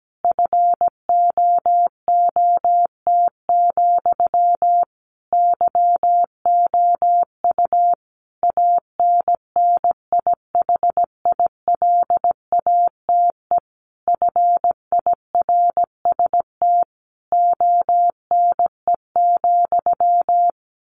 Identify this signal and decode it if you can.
Morse code, heard as a keyed tone, about 17 wpm, 700 Hz. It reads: FOOT, YOU ANNIHILATE FIRST ONE,
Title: A Sound Of Thunder 17wpm